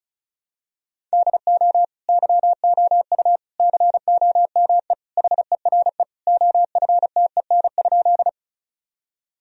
Morse code, heard as a keyed tone, about 34 wpm, 700 Hz. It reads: DO YOU COME HERE OFTEN?
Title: Sentences from Top 1000 Words - Review Part 1 - 35wpm